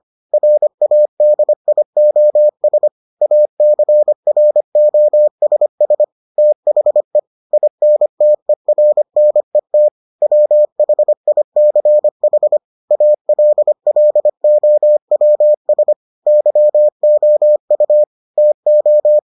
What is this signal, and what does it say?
Morse code, heard as a keyed tone, about 25 wpm, 600 Hz. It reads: RADIOS ACROSS THE INTERNET WHICH ALLOWS YOU TO